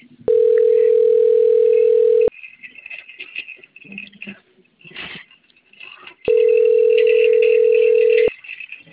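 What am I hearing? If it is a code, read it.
Ringback tone (North America), 440 + 480 Hz.
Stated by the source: Outgoing telephone call recorded on a phone system at Campinas.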